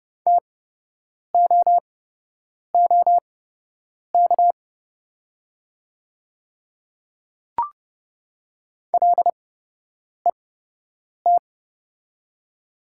Morse code, heard as a keyed tone, about 29 wpm, spaced as for 8 wpm, 700 Hz. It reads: TOOK LET
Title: 8x Character Spacing - Top 300 Words - 30wpm